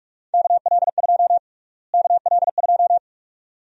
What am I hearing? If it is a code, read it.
Morse code, heard as a keyed tone, about 42 wpm, 700 Hz. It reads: KL2 KL2
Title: USA Call Sign Prefixes - Repeated 2x - 45wpm